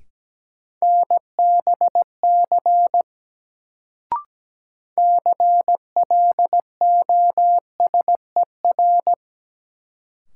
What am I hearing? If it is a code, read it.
Morse code, heard as a keyed tone, about 17 wpm, 700 Hz. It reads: NBC CLOSER
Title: Sets of 5 Words Encore - New Word Drill - 17wpm